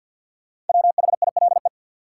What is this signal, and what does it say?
Morse code, heard as a keyed tone, about 53 wpm, 700 Hz. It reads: WHILE
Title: Sets of 3 Words - New Word Drill - 50wpm